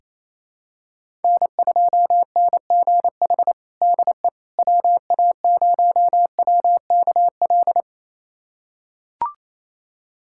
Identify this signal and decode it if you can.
Morse code, heard as a keyed tone, about 28 wpm, 700 Hz. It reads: N2NGH DE WA0WKL
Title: USA Callsign DE Callsign 28wpm